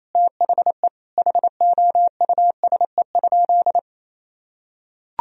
Morse code, heard as a keyed tone, about 28 wpm, 700 Hz. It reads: THE HOUSE?
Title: Sentences from Top 1000 Words - Review Part 2 - 28wpm